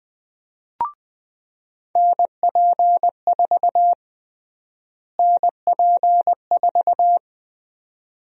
Morse code, heard as a keyed tone, about 20 wpm, 700 Hz. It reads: NP4 NP4
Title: USA Call Sign Prefixes - Repeated 2x - 20wpm